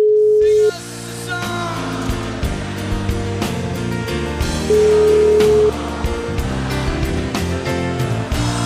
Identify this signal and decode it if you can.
Ringback tone (Europe, 425 Hz), with noise.